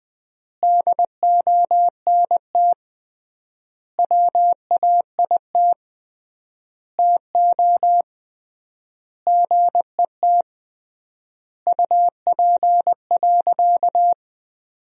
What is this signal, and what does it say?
Morse code, heard as a keyed tone, about 21 wpm, 700 Hz. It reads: DONT WAIT TO GET UP.
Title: 3x Word-Spacing - Sentences from Top 500 Words - 20wpm